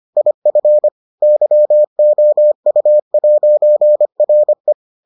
Morse code, heard as a keyed tone, about 25 wpm, 600 Hz. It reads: IF YOU'RE